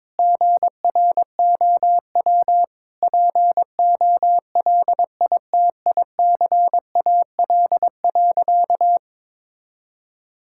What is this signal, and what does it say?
Morse code, heard as a keyed tone, about 22 wpm, 700 Hz. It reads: GROW POLITICAL.
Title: Sentences from Top 1600 Words 22wpm